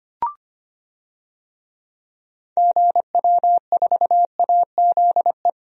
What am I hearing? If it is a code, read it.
Morse code, heard as a keyed tone, about 25 wpm, 700 Hz. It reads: GW4AZE
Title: DXCC-100 Call Signs - 6 Characters - 25wpm